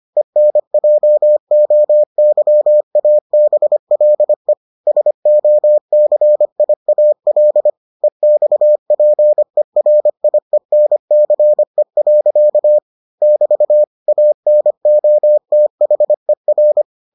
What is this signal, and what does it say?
Morse code, heard as a keyed tone, about 25 wpm, 600 Hz. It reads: ENJOYABLE SOCIAL EXPERIENCE. <BT> ANOTHER